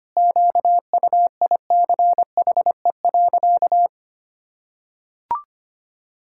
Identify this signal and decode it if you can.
Morse code, heard as a keyed tone, about 25 wpm, 700 Hz. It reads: QUICHE.
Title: Pangram Challenge v2 - 25wpm